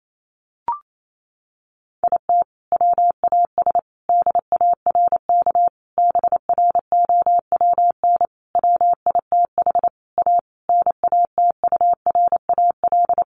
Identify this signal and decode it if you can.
Morse code, heard as a keyed tone, about 28 wpm, 700 Hz. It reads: IT WAS DARK BROWN WITH A NATURAL
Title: Sentences from Top 1400 Words 28wpm